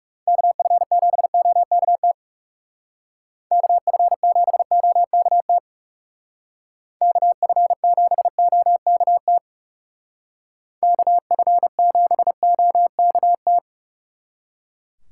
Morse code, heard as a keyed tone, about 36 wpm, 700 Hz. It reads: KF7OKT KF7OKT KF7OKT KF7OKT
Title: Speed Racing - USA Call Signs 30wpm